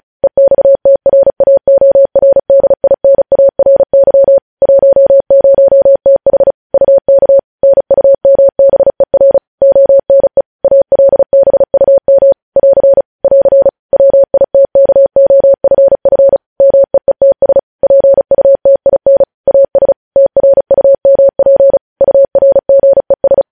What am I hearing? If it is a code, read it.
Morse code, heard as a keyed tone, about 36 wpm, 550 Hz. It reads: EXTRAORDINARY 10TH UK NUMBER ONE ALBUM <AR> <AR> WITKOFF MEETS PUTIN AS TRUMP URGES
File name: podcast / News headlines-1x-35x35